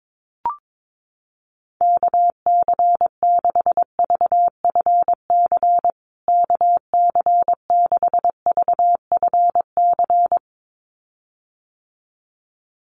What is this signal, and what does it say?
Morse code, heard as a keyed tone, about 22 wpm, 700 Hz. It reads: KC6VFC KC6VFC
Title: USA Call Signs Repeated 2x 22wpm